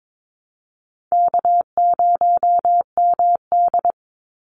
Morse code, heard as a keyed tone, about 22 wpm, 700 Hz. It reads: K0MD
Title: Call Signs - 22wpm